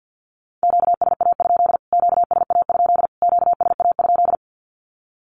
Morse code, heard as a keyed tone, about 48 wpm, 700 Hz. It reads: QSA? QSA? QSA?
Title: Q-Codes Repeated 3x 50wpm